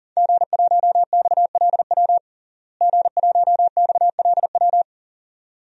Morse code, heard as a keyed tone, about 40 wpm, 700 Hz. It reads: G1XLW G1XLW
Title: DXCC-100 Call Signs - 5 Characters - Repeated 2x - 40wpm